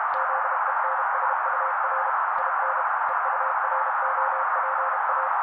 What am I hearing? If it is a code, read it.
Morse code, heard as a keyed tone, about 32 wpm, 530 Hz. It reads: ZENSUR IN EUROPA
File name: daily / 31wpm noise